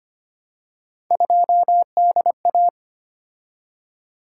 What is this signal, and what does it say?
Morse code, heard as a keyed tone, about 25 wpm, 700 Hz. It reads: 2DA
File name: Three Letter-Number Combinations - 25wpm